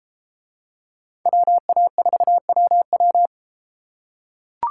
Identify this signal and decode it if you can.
Morse code, heard as a keyed tone, about 33 wpm, 700 Hz. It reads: WA4WW